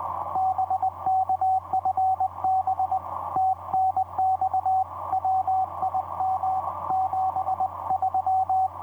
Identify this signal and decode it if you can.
Morse code, heard as a keyed tone, about 20 wpm, 760 Hz, with radio noise. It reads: BK FB TNX WIM 73
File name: vp8cmh-mm on5jy etc 30m cw 23dec2010 2206